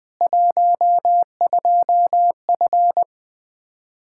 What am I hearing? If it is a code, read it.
Morse code, heard as a keyed tone, about 21 wpm, 700 Hz. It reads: 12F
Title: K1USN SST QSO Practice 20wpm